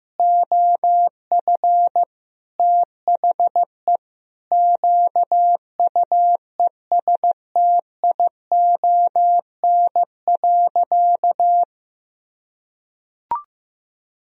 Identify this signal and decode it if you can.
Morse code, heard as a keyed tone, about 16 wpm, 700 Hz. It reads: OF THE QUESTION.
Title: Sentences from Top 400 Words and Abbreviations 15wpm